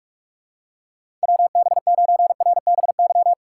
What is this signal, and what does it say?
Morse code, heard as a keyed tone, about 46 wpm, 700 Hz. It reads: WB9RBY
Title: USA Call Signs - 6 Characters - 3x Character Spacing - 45wpm